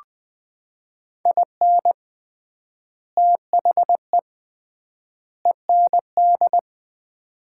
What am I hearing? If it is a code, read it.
Morse code, heard as a keyed tone, about 20 wpm, 700 Hz. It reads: IN THE END
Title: 3x Word-Spacing - Sets of 3 Words - 20wpm